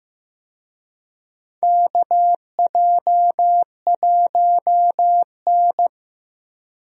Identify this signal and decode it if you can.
Morse code, heard as a keyed tone, about 15 wpm, 700 Hz. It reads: KJ1N
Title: USA Call Signs - 4 Characters - Repeated 3x - 15wpm